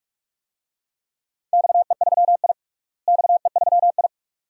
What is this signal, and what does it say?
Morse code, heard as a keyed tone, about 42 wpm, 700 Hz. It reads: XE3I XE3I
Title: DXCC-100 Call Signs - 4 Characters - Repeated 2x - 45wpm